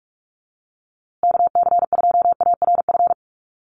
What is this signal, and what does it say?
Morse code, heard as a keyed tone, about 44 wpm, 700 Hz. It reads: KC2ARF
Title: USA Call Signs Repeated 2x 45wpm